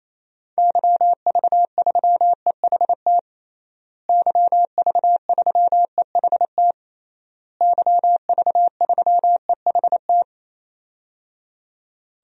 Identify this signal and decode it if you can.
Morse code, heard as a keyed tone, about 28 wpm, 700 Hz. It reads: YV3EHT YV3EHT YV3EHT
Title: DXCC-100 Call Signs - 6 Characters - Repeated 3x - 28wpm